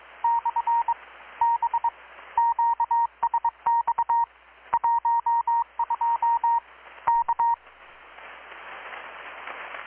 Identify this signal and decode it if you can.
Morse code, heard as a keyed tone, about 21 wpm, 930 Hz. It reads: / B QSX 12 K
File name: blog / swl / 6-abril-2019-12695-USB-KPS